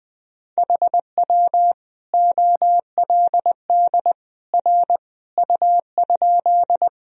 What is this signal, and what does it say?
Morse code, heard as a keyed tone, about 20 wpm, 700 Hz. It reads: HW OLD R U?
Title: Sentences from Top 200 Words and Abbreviations 20wpm